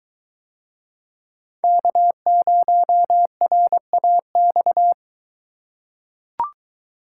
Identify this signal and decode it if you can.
Morse code, heard as a keyed tone, about 23 wpm, 700 Hz. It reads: K0RAX